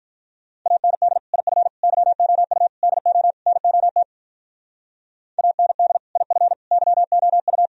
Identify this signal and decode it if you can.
Morse code, heard as a keyed tone, about 42 wpm, 700 Hz. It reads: AND IF YOU DO NOT AND IF YOU
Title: Speed-Racing - Sets of 5 Words 35wpm